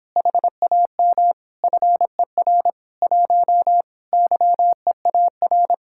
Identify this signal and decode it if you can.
Morse code, heard as a keyed tone, about 26 wpm, 700 Hz. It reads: HAM FER 1 YEAR